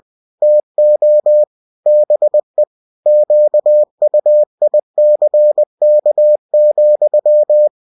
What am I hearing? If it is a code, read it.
Morse code, heard as a keyed tone, about 20 wpm, 600 Hz. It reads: TO BE QUICK,
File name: cw-books / winnie-the-pooh / winnie-20wpm-page-0087